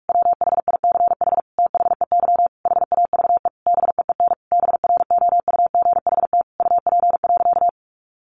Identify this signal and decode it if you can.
Morse code, heard as a keyed tone, about 44 wpm, 700 Hz. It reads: WHICH THEY HAVE BEEN BROUGHT UP.
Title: Sentences from Top 1500 Words 45wpm